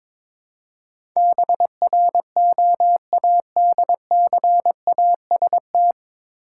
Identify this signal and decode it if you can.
Morse code, heard as a keyed tone, about 22 wpm, 700 Hz. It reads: BROADCAST
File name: 9 Letter Words - First 3 Letters Given - 22wpm